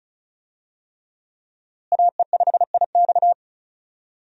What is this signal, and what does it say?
Morse code, heard as a keyed tone, about 34 wpm, 700 Hz. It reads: AE5IX
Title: USA Call Signs - 5 Characters - Repeated 3x - 35wpm